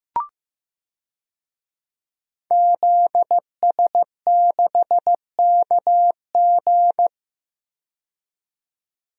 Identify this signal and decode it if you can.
Morse code, heard as a keyed tone, about 15 wpm, 700 Hz. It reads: ZS6KG
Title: DXCC-100 Call Signs - 5 Characters - 15wpm